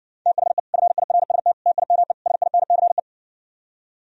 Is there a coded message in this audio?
Yes: Morse code from 0.3 to 3.0 s, heard as a keyed tone, about 60 wpm, 700 Hz. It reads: THE FIRST TIME SINCE